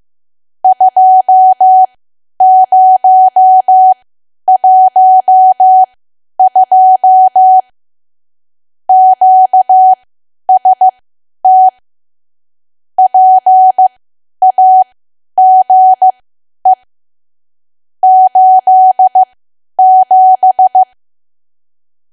Morse code, heard as a keyed tone, about 15 wpm, spaced as for 10 wpm, 750 Hz. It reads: 2012 QST PAGE 87
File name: cwa / W1AW-Level2 / 10WPM